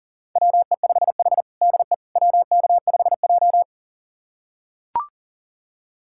Morse code, heard as a keyed tone, about 37 wpm, 700 Hz. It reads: WE5H DE WK5J
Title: USA Callsign DE Callsign - 4 Characters - 40wpm